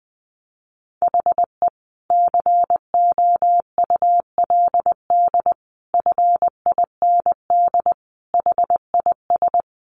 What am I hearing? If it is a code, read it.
Morse code, heard as a keyed tone, about 20 wpm, 700 Hz. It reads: HE COULD FIND HIS
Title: Sentences from Top 1400 Words 20wpm